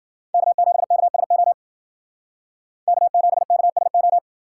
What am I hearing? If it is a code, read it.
Morse code, heard as a keyed tone, about 51 wpm, 700 Hz. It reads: K7KSO K7KSO